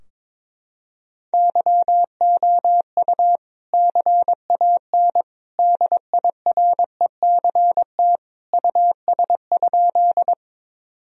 Morse code, heard as a keyed tone, about 22 wpm, 700 Hz. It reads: YOU CAN DIRECT US?
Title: Sentences from Top 1000 Words - Review Part 2 - 22wpm